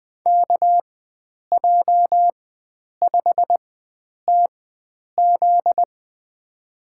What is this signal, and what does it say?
Morse code, heard as a keyed tone, about 20 wpm, spaced as for 9 wpm, 700 Hz. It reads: KJ5TZ
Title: USA Call Signs - 5 Characters - 4x Character Spacing - 20wpm